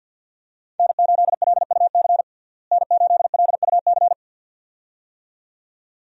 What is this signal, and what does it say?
Morse code, heard as a keyed tone, about 49 wpm, 700 Hz. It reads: N8LUC N8LUC
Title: USA Call Signs - 5 Characters - Repeated 2x - 50wpm